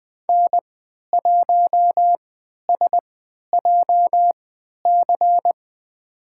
Morse code, heard as a keyed tone, about 20 wpm, spaced as for 11 wpm, 700 Hz. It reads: N1SJC
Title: USA Call Signs - 5 Characters - 3x Character Spacing - 20wpm